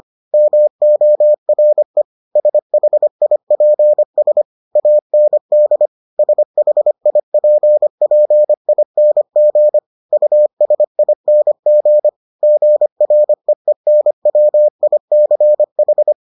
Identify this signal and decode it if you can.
Morse code, heard as a keyed tone, about 25 wpm, 600 Hz. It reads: MORE SHIPS AND SHIPPING USING GREENWICH